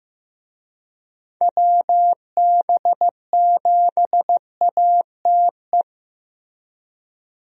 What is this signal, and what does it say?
Morse code, heard as a keyed tone, about 15 wpm, 700 Hz. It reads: WB7ATE
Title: USA Call Signs - 6 Characters - Repeated 2x - 15wpm